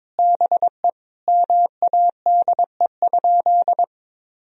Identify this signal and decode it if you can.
Morse code, heard as a keyed tone, about 22 wpm, 700 Hz. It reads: BE MADE?
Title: Sentences from Top 1000 Words - Review Part 4 - 22wpm